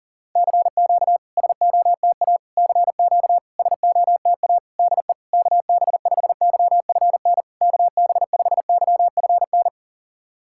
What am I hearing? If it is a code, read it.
Morse code, heard as a keyed tone, about 40 wpm, 700 Hz. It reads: CQ SOTA CQ SOTA DE KB5YFN KB5YFN
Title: SOTA QSO Practice 40wpm